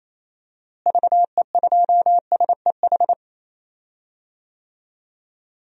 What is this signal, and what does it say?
Morse code, heard as a keyed tone, about 28 wpm, 700 Hz. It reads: VE2SEH
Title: DXCC-100 Call Signs - 6 Characters - Repeated 2x - 28wpm